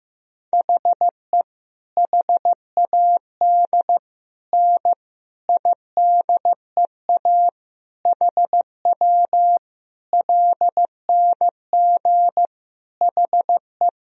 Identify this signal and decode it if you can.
Morse code, heard as a keyed tone, about 15 wpm, 700 Hz. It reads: HE HAD N IDEA HW LNG HE
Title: Sentences from Top 300 Words and Abbreviations 15wpm